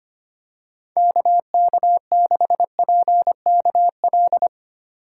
Morse code, heard as a keyed tone, about 25 wpm, 700 Hz. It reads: KK6PKL
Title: USA Call Signs - 6 Characters - 8x Character Spacing - 25wpm